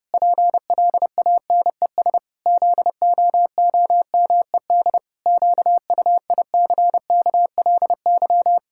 Morse code, heard as a keyed tone, about 32 wpm, 700 Hz. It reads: PLANES ZOOMED QUICKLY
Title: Pangram Challenge v2 - 30wpm